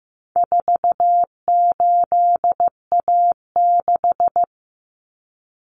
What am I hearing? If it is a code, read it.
Morse code, heard as a keyed tone, about 15 wpm, 700 Hz. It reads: 48A6
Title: Common Cut Numbers - 4-Digit 15wpm